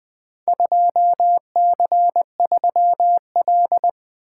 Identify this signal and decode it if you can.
Morse code, heard as a keyed tone, about 20 wpm, 700 Hz. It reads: 2C3L
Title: Four Letter-Number Combinations - 20wpm